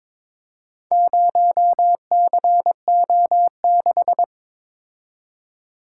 Morse code, heard as a keyed tone, about 23 wpm, 690 Hz. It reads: 0CO6
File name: practice / 22wpm / Four Letter-Number Combinations - 22wpm